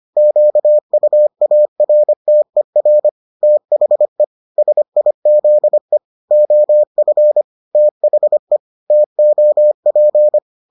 Morse code, heard as a keyed tone, about 25 wpm, 600 Hz. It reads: QUARTER THE SIZE OF THE TOP